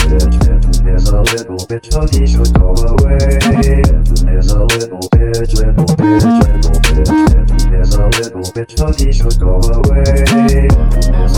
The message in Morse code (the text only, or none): none